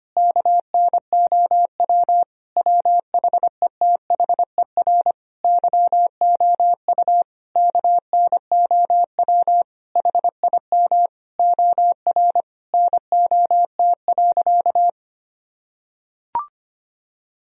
Morse code, heard as a keyed tone, about 25 wpm, 700 Hz. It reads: KNOW WHETHER YOU KNOW HIM OR NOT.
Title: Sentences from Top 1000 Words - Review Part 3 - 25wpm